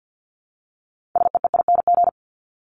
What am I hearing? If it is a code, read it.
Morse code, heard as a keyed tone, about 49 wpm, 700 Hz. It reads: SEEING
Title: ING Suffix - Top 100 Words with Base Words 50wpm